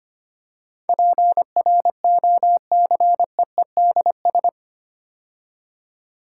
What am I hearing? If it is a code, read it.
Morse code, heard as a keyed tone, about 25 wpm, 700 Hz. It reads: PROCEEDS
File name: 8 Letter Words - First 3 Letters Given - 25wpm